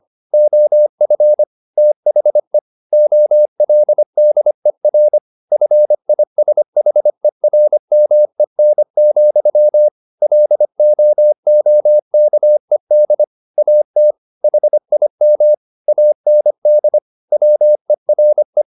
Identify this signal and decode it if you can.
Morse code, heard as a keyed tone, about 25 wpm, 600 Hz. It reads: OF THE OLDER FISHERMEN, LOOKED AT HIM AND WERE
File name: the old man and the sea part 1 25wpm